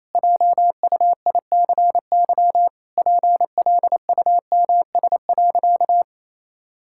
Morse code, heard as a keyed tone, about 28 wpm, 700 Hz. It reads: JUICY PLUMS.
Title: Pangram Challenge v2 - 28wpm